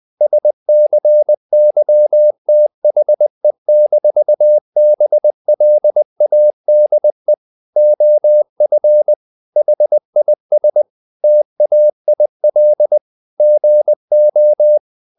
Morse code, heard as a keyed tone, about 20 wpm, 600 Hz. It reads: SCYTHE-BLADE OF HIS TAIL GO
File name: the old man and the sea / the old man and the sea part 1 20wpm